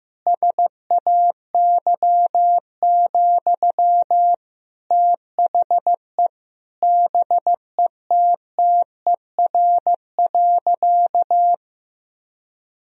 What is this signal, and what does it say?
Morse code, heard as a keyed tone, about 15 wpm, 700 Hz. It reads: SAY, THE BETTER.
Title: Sentences from Top 1000 Words - Review Part 2 - 15wpm